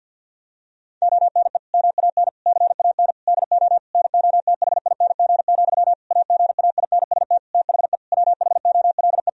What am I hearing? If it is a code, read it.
Morse code, heard as a keyed tone, about 50 wpm, 700 Hz. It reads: ONE MAN CAN DO NOTHING, AGAINST THE WHOLE
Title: Sentences from Top 1000 Words - Review Part 4 - 50wpm